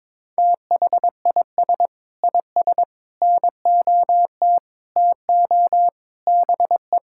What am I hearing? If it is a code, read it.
Morse code, heard as a keyed tone, about 21 wpm, 700 Hz. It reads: THIS IS NOT TO BE
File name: Sentences with 7-Word Phrases - 22wpm